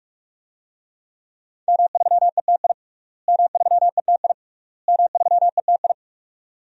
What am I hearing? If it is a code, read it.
Morse code, heard as a keyed tone, about 45 wpm, 700 Hz. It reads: M3ETI M3ETI M3ETI
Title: DXCC-100 Call Signs - 5 Characters - Repeated 3x - 45wpm